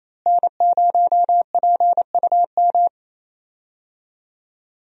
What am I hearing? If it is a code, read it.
Morse code, heard as a keyed tone, about 28 wpm, 700 Hz. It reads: N0PUM
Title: USA Call Signs - 5 Characters - Repeated 2x - 28wpm